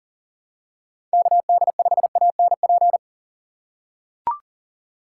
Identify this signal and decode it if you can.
Morse code, heard as a keyed tone, about 41 wpm, 700 Hz. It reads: KD5ANP
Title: USA Call Signs - 6 Characters - 8x Character Spacing - 40wpm